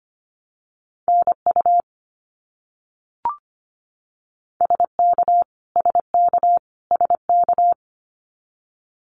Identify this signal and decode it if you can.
Morse code, heard as a keyed tone, about 25 wpm, 700 Hz. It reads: NU SK SK SK
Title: Canadian Provinces and Territories - Repeated 3x - 25wpm